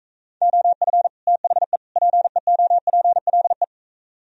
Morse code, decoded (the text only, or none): OF THE PEOPLE